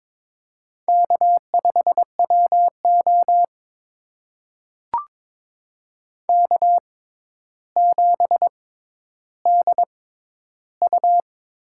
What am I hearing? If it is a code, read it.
Morse code, heard as a keyed tone, about 22 wpm, spaced as for 11 wpm, 700 Hz. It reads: K5WO K7DU
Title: USA Call Signs - 4 Characters - 6x Character Spacing - 22wpm